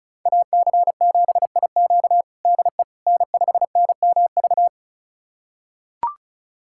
Morse code, heard as a keyed tone, about 36 wpm, 700 Hz. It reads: AC7IQ DE N5NMV
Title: USA Callsign DE Callsign - 5 Characters - 35wpm